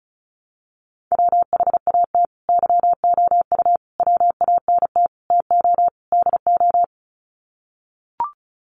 Morse code, heard as a keyed tone, about 34 wpm, 700 Hz. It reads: WHAT YOU WANT TO DO
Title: Sets of 5 Words Encore 35wpm